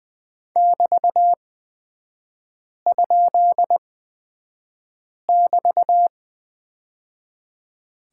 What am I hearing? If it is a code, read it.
Morse code, heard as a keyed tone, about 20 wpm, spaced as for 5 wpm, 700 Hz. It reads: <BT>?<BT>
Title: 224 Lesson 36e - BT - Three Characters 20wpm